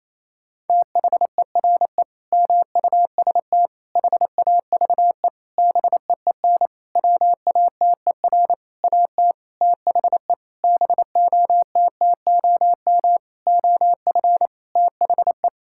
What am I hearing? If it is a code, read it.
Morse code, heard as a keyed tone, about 28 wpm, 700 Hz. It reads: THERE MUST HAVE BEEN WATER AT THE BOTTOM OF THE
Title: Sentences from Top 1000 Words - Review Part 6 - 28wpm